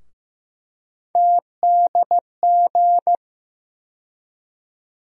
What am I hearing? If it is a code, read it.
Morse code, heard as a keyed tone, about 15 wpm, 700 Hz. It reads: TDG
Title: USA Call Sign Postfixes - 3 Letters 15wpm